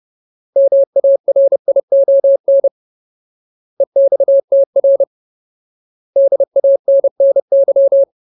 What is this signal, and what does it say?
Morse code, heard as a keyed tone, about 30 wpm, 550 Hz. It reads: MARION EXTR DANNY
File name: Ultimate Word List / 30WPM Ultimate